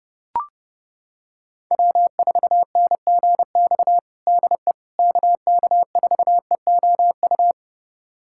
Morse code, heard as a keyed tone, about 30 wpm, 700 Hz. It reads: W4NGX DE KK4EOU
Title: USA Callsign DE Callsign 30wpm